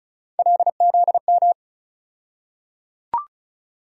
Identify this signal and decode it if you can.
Morse code, heard as a keyed tone, about 34 wpm, 700 Hz. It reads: LZM
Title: USA Call Sign Postfixes - 3 Letters 35wpm